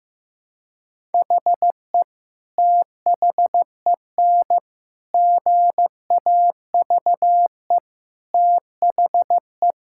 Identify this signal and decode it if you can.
Morse code, heard as a keyed tone, about 15 wpm, 700 Hz. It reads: HE THEN GAVE THE
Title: Sentences from Top 1000 Words - Review Part 6 - 15wpm